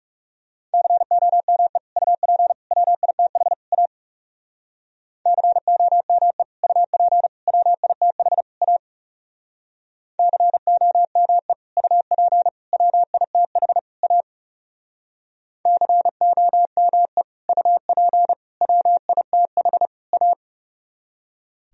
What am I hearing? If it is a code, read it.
Morse code, heard as a keyed tone, about 37 wpm, 700 Hz. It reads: COME UP WITH A COME UP WITH A COME UP WITH A COME UP WITH A
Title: Speed-Racing - Sets of 4 Words 30wpm